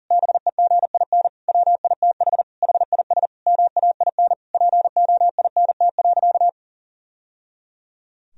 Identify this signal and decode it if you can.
Morse code, heard as a keyed tone, about 40 wpm, 700 Hz. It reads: BEGIN WITH HIS MAIN POINT.
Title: Sentences from Top 1000 Words - Review Part 2 - 40wpm